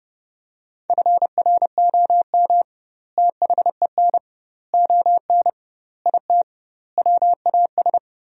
Morse code, heard as a keyed tone, about 32 wpm, 700 Hz. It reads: FROM THEN ON IT WAS
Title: 2x Word-Spacing - Sentences from Top 400 Words - 30wpm